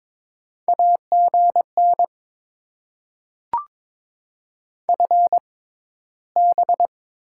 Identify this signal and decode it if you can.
Morse code, heard as a keyed tone, about 22 wpm, spaced as for 14 wpm, 700 Hz. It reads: AGN FB
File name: Top 20 QSO Elements - 6x Character Spacing - 22wpm